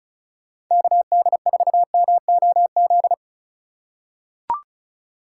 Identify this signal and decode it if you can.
Morse code, heard as a keyed tone, about 35 wpm, 700 Hz. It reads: KD4MOZ
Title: USA Call Signs - 6 Characters - 3x Character Spacing - 35wpm